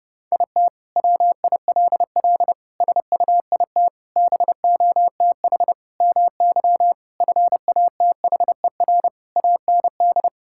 Morse code, heard as a keyed tone, about 32 wpm, 700 Hz. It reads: IT WILL SUIT BOTH MY FATHER AND
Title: Sentences from Top 1600 Words 30wpm